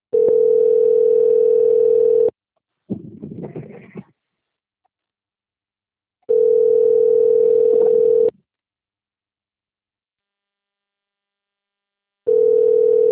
Ringback tone (North America), 440 + 480 Hz.